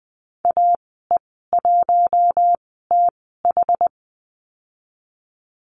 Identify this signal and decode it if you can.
Morse code, heard as a keyed tone, about 20 wpm, spaced as for 15 wpm, 700 Hz. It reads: AE1TH
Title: 088 Lesson 15g - Number 5 - Call Signs 20wpm